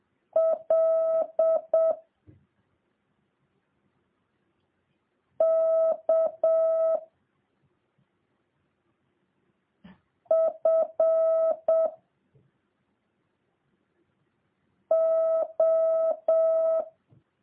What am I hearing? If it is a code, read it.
Morse code, heard as a keyed tone, about 7 wpm, 650 Hz. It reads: L K F O